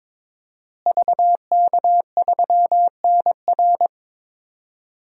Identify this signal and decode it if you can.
Morse code, heard as a keyed tone, about 22 wpm, 700 Hz. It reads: VK3NR
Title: DXCC-100 Call Signs - 5 Characters - Repeated 3x - 22wpm